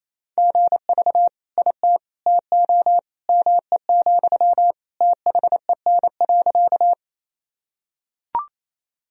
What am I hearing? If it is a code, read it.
Morse code, heard as a keyed tone, about 28 wpm, 700 Hz. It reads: GV IT TO ME, THEN.
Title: Sentences from Top 200 Words and Abbreviations 28wpm